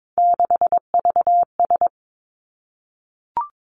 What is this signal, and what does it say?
Morse code, heard as a keyed tone, about 22 wpm, 700 Hz. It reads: 6VS
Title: Three Challenging Letter-Number Combinations 22wpm